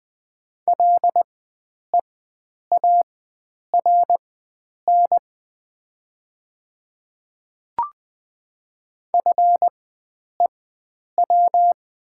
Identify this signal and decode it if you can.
Morse code, heard as a keyed tone, about 20 wpm, spaced as for 9 wpm, 700 Hz. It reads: LEARN FEW
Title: 4x Character Spacing - Top 300 Words - 20wpm